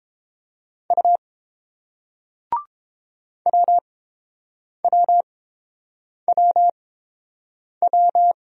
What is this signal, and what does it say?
Morse code, heard as a keyed tone, about 30 wpm, spaced as for 7 wpm, 700 Hz. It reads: U WWWW